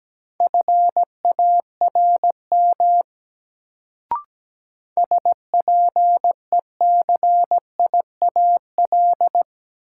Morse code, heard as a keyed tone, about 17 wpm, 700 Hz. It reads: FARM SPECIAL
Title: Sentences from Top 1200 Words - New Word Drill Alt - 17wpm